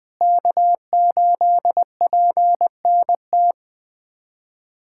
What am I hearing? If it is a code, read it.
Morse code, heard as a keyed tone, about 21 wpm, 700 Hz. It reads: K8PNT
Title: USA Callsign DE Callsign - 5 Characters - 20wpm